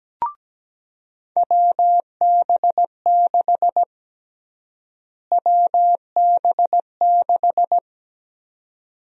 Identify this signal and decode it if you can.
Morse code, heard as a keyed tone, about 17 wpm, 700 Hz. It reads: WB6 WB6
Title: USA Call Sign Prefixes - Repeated 2x - 17wpm